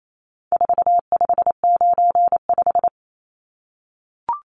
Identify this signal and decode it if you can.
Morse code, heard as a keyed tone, about 28 wpm, 700 Hz. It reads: 4595
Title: Numbers - 4-Digit 28wpm